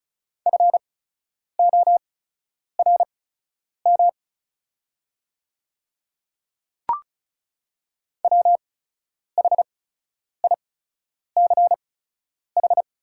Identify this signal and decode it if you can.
Morse code, heard as a keyed tone, about 34 wpm, spaced as for 10 wpm, 700 Hz. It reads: FORM WHICH
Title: 8x Character Spacing - Top 200 Words - 35wpm